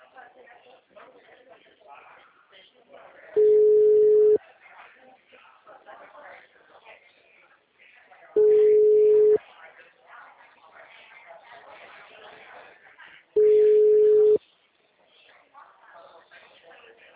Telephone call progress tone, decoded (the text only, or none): ringback tone